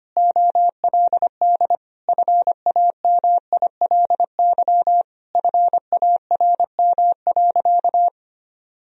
Morse code, heard as a keyed tone, about 25 wpm, 700 Hz. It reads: OLD FAMILY FARM.
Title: Sentences from Top 1000 Words - Review Part 2 - 25wpm